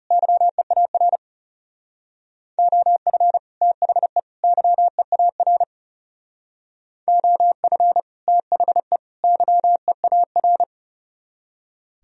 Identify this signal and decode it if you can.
Morse code, heard as a keyed tone, about 37 wpm, 700 Hz. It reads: YEAR OF THE YEAR OF THE YEAR
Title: Speed-Racing - Sets of 3 Words 30wpm